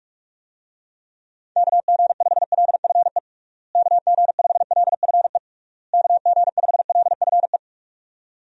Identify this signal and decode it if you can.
Morse code, heard as a keyed tone, about 46 wpm, 700 Hz. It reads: KG5LFE KG5LFE KG5LFE
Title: USA Call Signs - 6 Characters - Repeated 3x - 45wpm